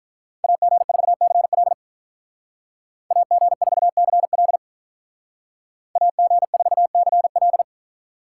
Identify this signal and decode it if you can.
Morse code, heard as a keyed tone, about 46 wpm, 700 Hz. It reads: AG4CL AG4CL AG4CL